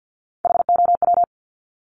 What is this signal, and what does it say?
Morse code, heard as a keyed tone, about 50 wpm, 700 Hz. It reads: HOW
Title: Multi-letter Phonemes 50wpm